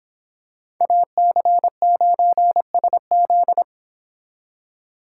Morse code, heard as a keyed tone, about 26 wpm, 700 Hz. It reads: AC9SZ